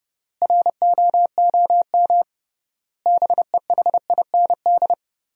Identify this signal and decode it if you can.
Morse code, heard as a keyed tone, about 30 wpm, 700 Hz. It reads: ROOM BEHIND
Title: 3x Word-Spacing - Sentences from Top 500 Words - 30wpm